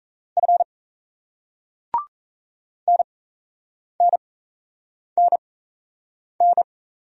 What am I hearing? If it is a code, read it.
Morse code, heard as a keyed tone, about 37 wpm, spaced as for 8 wpm, 700 Hz. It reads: F NNNN